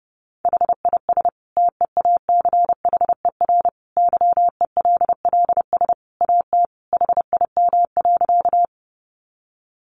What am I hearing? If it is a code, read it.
Morse code, heard as a keyed tone, about 30 wpm, 700 Hz. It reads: HIS TEACHER YELLS AT HIM.
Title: Additional Easy Phrases 30wpm